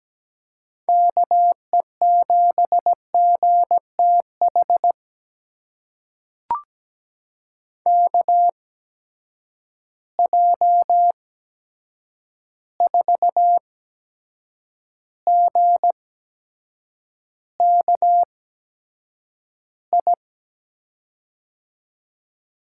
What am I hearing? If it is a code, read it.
Morse code, heard as a keyed tone, about 17 wpm, spaced as for 7 wpm, 700 Hz. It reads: KE7GTH KJ4GKI
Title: USA Call Signs - 8x Character Spacing - 17wpm